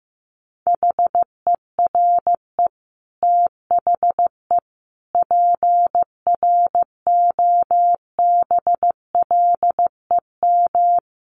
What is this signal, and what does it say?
Morse code, heard as a keyed tone, about 15 wpm, 700 Hz. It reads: HERE THE PROBLEM
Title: Sentences from Top 1200 Words 15wpm